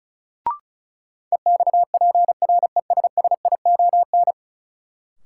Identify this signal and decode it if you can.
Morse code, heard as a keyed tone, about 34 wpm, 700 Hz. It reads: EXPRESSION
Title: Sentences from Top 1200 Words - New Word Drill - 35wpm